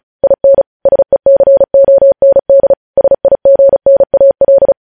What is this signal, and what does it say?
Morse code, heard as a keyed tone, about 36 wpm, 550 Hz. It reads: IN SECOND SIGNAL
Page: news headlines in morse code at 35 wpm